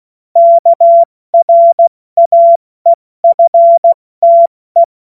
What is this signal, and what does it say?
Morse code, heard as a keyed tone, about 16 wpm, 680 Hz. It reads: KRAEFTE
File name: Morsekurs / Witze / Witz 11. Januar 2022, 70 F80